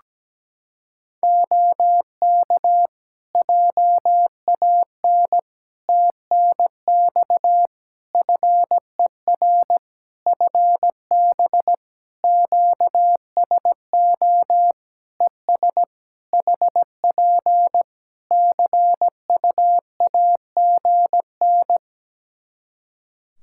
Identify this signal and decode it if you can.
Morse code, heard as a keyed tone, about 17 wpm, 700 Hz. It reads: OK JAN TNX FER FB QSO ES HP CUAGN
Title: K1USN SST QSO Practice 17wpm